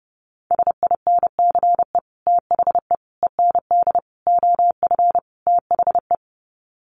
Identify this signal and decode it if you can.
Morse code, heard as a keyed tone, about 30 wpm, 700 Hz. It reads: SINCE THE END OF THE
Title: Sets of 5 Words Encore 30wpm